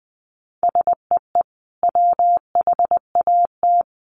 Morse code, heard as a keyed tone, about 20 wpm, 700 Hz. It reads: SEE WHAT
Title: Sentences from Top 500 Words 20wpm